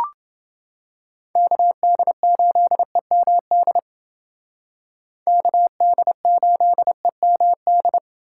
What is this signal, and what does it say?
Morse code, heard as a keyed tone, about 29 wpm, 700 Hz. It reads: KD8EMD KD8EMD